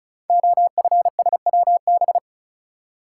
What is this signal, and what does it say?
Morse code, heard as a keyed tone, about 36 wpm, 700 Hz. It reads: OFSWB
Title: Code Groups - 5 Letters - 35wpm